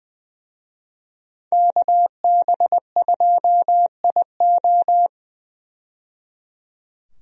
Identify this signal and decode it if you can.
Morse code, heard as a keyed tone, about 20 wpm, 700 Hz. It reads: KB2IO
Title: USA Call Signs - 5 Characters - 20wpm